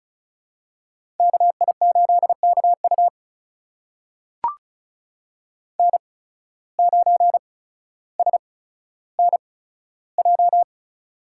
Morse code, heard as a keyed tone, about 36 wpm, spaced as for 15 wpm, 700 Hz. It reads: KI8KU N9SNJ
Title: USA Call Signs - 8x Character Spacing - 35wpm